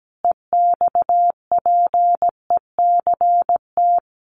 Morse code, heard as a keyed tone, about 17 wpm, 700 Hz. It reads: EXPECT
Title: Sentences from Top 1100 Words 17wpm